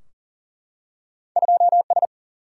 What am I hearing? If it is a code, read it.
Morse code, heard as a keyed tone, about 43 wpm, 700 Hz. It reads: 2 S
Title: Single-Letter Single-Number 40wpm